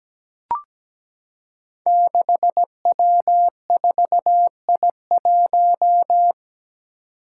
Morse code, heard as a keyed tone, about 17 wpm, 700 Hz. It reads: 6W4I1
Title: Five Challenging Letter-Number Combinations 17wpm